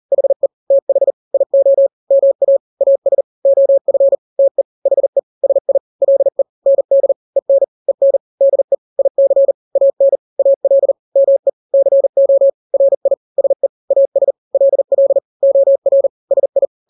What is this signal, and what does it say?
Morse code, heard as a keyed tone, about 40 wpm, 550 Hz. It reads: HE TH IO MA AS OF TE HE SI LE ND EN EN DE IC AN AL ME CO RI SE AS LL OR SI
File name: n-grams / 2-grams 40 35 WPM